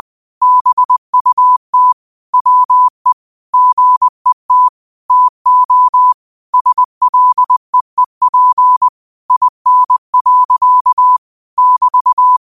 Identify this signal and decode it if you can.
Morse code, heard as a keyed tone, about 20 wpm, 1000 Hz. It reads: BUT WE GET TO SLEEP IN. <BT>
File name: qotdfiles / 20wpm